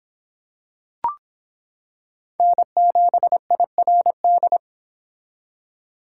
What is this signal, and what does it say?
Morse code, heard as a keyed tone, about 25 wpm, 700 Hz. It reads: N7IRD